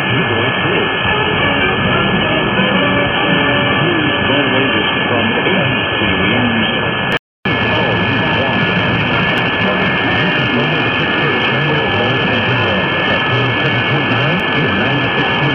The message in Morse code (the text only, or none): none